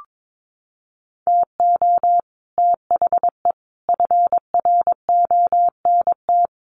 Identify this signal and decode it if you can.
Morse code, heard as a keyed tone, about 22 wpm, 700 Hz. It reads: TO THE FRONT
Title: Sets of 5 Words 22wpm